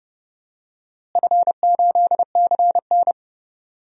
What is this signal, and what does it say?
Morse code, heard as a keyed tone, about 30 wpm, 700 Hz. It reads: F8CN
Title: DXCC-100 Call Signs - 4 Characters - Repeated 3x - 30wpm